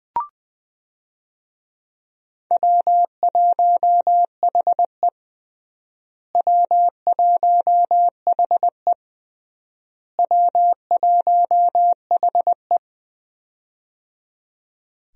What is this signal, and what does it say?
Morse code, heard as a keyed tone, about 20 wpm, 700 Hz. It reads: W1HE W1HE W1HE
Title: USA Call Signs - 4 Characters - Repeated 3x - 20wpm